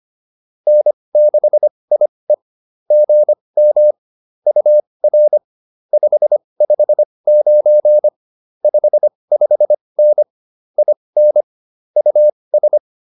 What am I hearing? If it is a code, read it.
Morse code, heard as a keyed tone, about 25 wpm, spaced as for 20 wpm, 600 Hz. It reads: N6IE GM UR 559 55N IN US